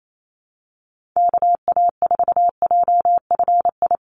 Morse code, heard as a keyed tone, about 28 wpm, 700 Hz. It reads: KA4JFI
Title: USA Call Signs 28wpm